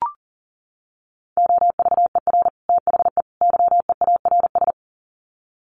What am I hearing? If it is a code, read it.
Morse code, heard as a keyed tone, about 40 wpm, 700 Hz. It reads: OVER THE YEARS
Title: Sets of 3 Words 40wpm